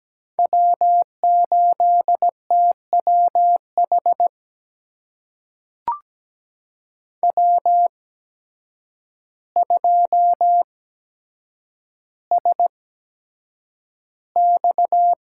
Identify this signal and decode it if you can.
Morse code, heard as a keyed tone, about 17 wpm, spaced as for 8 wpm, 700 Hz. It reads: W8TWH W2SX
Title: USA Call Signs - 8x Character Spacing - 17wpm